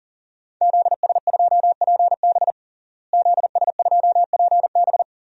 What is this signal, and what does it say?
Morse code, heard as a keyed tone, about 40 wpm, 700 Hz. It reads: ZS2PB ZS2PB
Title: DXCC-100 Call Signs - 5 Characters - Repeated 2x - 40wpm